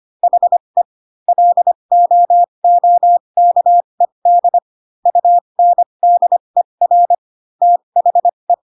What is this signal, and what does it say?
Morse code, heard as a keyed tone, about 25 wpm, spaced as for 22 wpm, 700 Hz. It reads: HE LOOKED UNDER THE